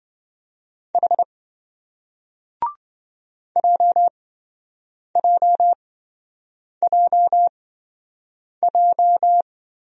Morse code, heard as a keyed tone, about 27 wpm, spaced as for 7 wpm, 700 Hz. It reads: H JJJJ